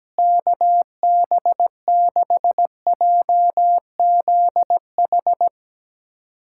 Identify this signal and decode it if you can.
Morse code, heard as a keyed tone, about 17 wpm, 700 Hz. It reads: KB6JZH
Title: USA Callsign DE Callsign - 6 Characters - 17wpm